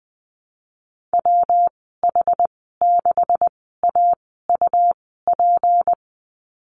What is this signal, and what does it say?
Morse code, heard as a keyed tone, about 20 wpm, spaced as for 14 wpm, 700 Hz. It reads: WH6AUP
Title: 231 Lesson 37g - Letter x - Call Signs 20wpm